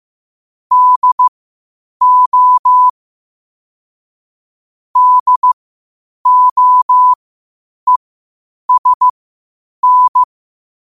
Morse code, heard as a keyed tone, about 15 wpm, spaced as for 8 wpm, 1000 Hz. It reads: DO DOESN
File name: qotdfiles / 8wpm